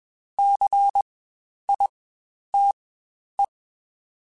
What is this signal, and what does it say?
Morse code, heard as a keyed tone, about 20 wpm, spaced as for 10 wpm, 800 Hz. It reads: CITE
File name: Spaced Alphas / Spaced Alphas mp3s / Cs sp @20